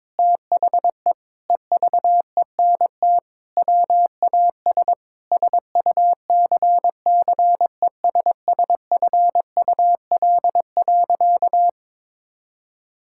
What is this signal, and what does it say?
Morse code, heard as a keyed tone, about 22 wpm, 700 Hz. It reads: THE EVENT WAS SUCCESSFUL.
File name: practice / 22wpm / Sentences with 7-Word Phrases - 22wpm